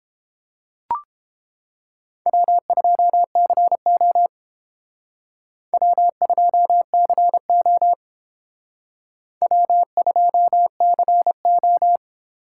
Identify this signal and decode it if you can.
Morse code, heard as a keyed tone, about 30 wpm, 700 Hz. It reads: W2CO W2CO W2CO